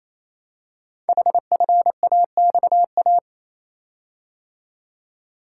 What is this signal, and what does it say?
Morse code, heard as a keyed tone, about 28 wpm, 700 Hz. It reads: HFAXA
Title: Code Groups - 5 Letters - 28wpm